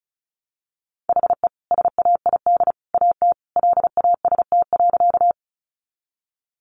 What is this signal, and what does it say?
Morse code, heard as a keyed tone, about 34 wpm, 700 Hz. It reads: HE SAID AT LAST.
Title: Sentences from Top 300 Words 35wpm